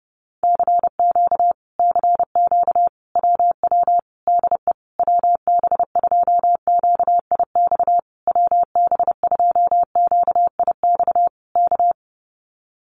Morse code, heard as a keyed tone, about 30 wpm, 700 Hz. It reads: CQ CQ WW DE WB2QIX WB2QIX K
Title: Common QSO Phrases Encore 30wpm